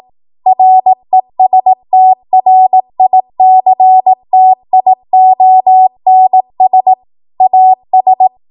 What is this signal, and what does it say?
Morse code, heard as a keyed tone, about 18 wpm, 750 Hz. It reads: RESTRICTIONS AS
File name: Morse / 18WPM